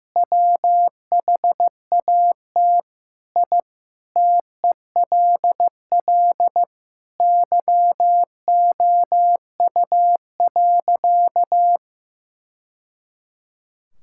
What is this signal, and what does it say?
Morse code, heard as a keyed tone, about 15 wpm, 700 Hz. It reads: WHAT I TELL YOU.
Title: Sentences from Top 1000 Words - Review Part 6 - 15wpm